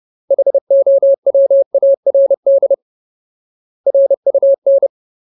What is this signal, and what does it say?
Morse code, heard as a keyed tone, about 30 wpm, 550 Hz. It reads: HOWARD RUN
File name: Ultimate Word List / 30WPM Ultimate